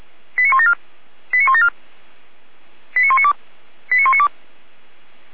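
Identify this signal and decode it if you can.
Five-tone selective call (ZVEI), heard as quick runs of five tones, tones of about 70 ms. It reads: 89174 89174 89192 89192
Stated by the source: ZVEI five-tone call 89340 received by SDR.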